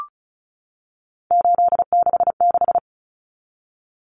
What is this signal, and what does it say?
Morse code, heard as a keyed tone, about 34 wpm, 700 Hz. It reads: 866
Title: Common Cut Numbers - 3-Digit 35wpm